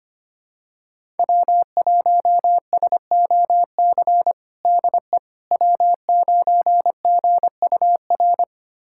Morse code, heard as a keyed tone, about 25 wpm, 700 Hz. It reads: W1SOC DE W9GUR
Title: USA Callsign DE Callsign - 5 Characters - 25wpm